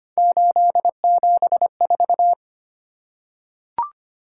Morse code, heard as a keyed tone, about 25 wpm, 700 Hz. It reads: 874
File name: Three Letter-Number Combinations - 25wpm